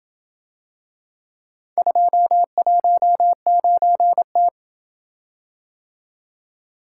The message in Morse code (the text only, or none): 219T